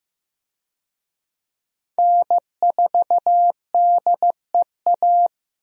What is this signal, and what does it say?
Morse code, heard as a keyed tone, about 15 wpm, 700 Hz. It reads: N4DEA
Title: USA Call Signs - 5 Characters - Repeated 3x - 15wpm